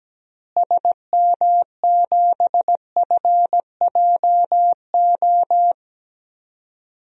Morse code, heard as a keyed tone, about 17 wpm, 700 Hz. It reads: SM7FJO
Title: DXCC-100 Call Signs - 6 Characters - Repeated 2x - 17wpm